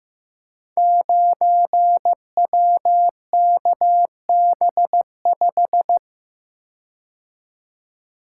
Morse code, heard as a keyed tone, about 15 wpm, 700 Hz. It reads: 9WKB5
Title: Code Groups - 5 Letter:Numbers - 15wpm